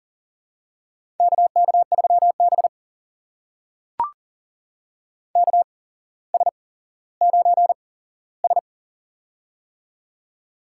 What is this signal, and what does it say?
Morse code, heard as a keyed tone, about 40 wpm, spaced as for 17 wpm, 700 Hz. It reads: KK3B KS9S
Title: USA Call Signs - 4 Characters - 8x Character Spacing - 40wpm